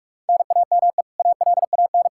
Morse code, heard as a keyed tone, about 46 wpm, 700 Hz. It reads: NAME ALAN
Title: Practice QSO - Exchange 1 - Beginning - 45wpm